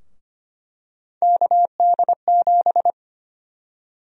Morse code, heard as a keyed tone, about 25 wpm, 700 Hz. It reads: KD7
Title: USA Call Sign Prefixes - Repeated 2x - 25wpm